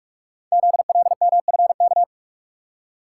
Morse code, heard as a keyed tone, about 44 wpm, 700 Hz. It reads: ZLMFK
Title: Code Groups - 5 Letters - 45wpm